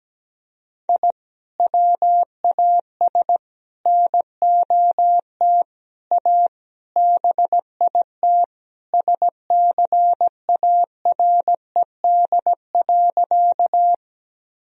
Morse code, heard as a keyed tone, about 17 wpm, 700 Hz. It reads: I WAS NOT A BIT SCARED.
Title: Sentences from Top 1000 Words - Review Part 4 - 17wpm